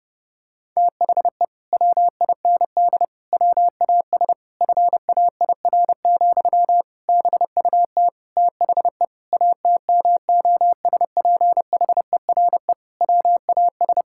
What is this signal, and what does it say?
Morse code, heard as a keyed tone, about 32 wpm, 700 Hz. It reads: THE WIND WAS FAIR, BUT THE ATMOSPHERE WAS
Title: Sentences from Top 1600 Words 30wpm